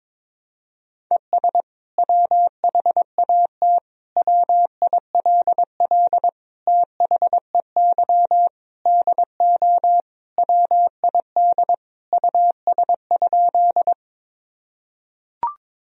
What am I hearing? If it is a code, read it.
Morse code, heard as a keyed tone, about 22 wpm, 700 Hz. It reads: ES WHAT WILL THEY DO WID US?
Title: Sentences from Top 200 Words and Abbreviations 22wpm